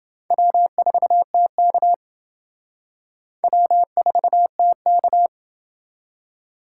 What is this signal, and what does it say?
Morse code, heard as a keyed tone, about 29 wpm, 700 Hz. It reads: W4TK W4TK